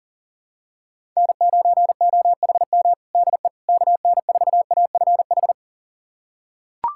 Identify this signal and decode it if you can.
Morse code, heard as a keyed tone, about 40 wpm, 700 Hz. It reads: N9OHM DE KN4AFH
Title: USA Callsign DE Callsign 40wpm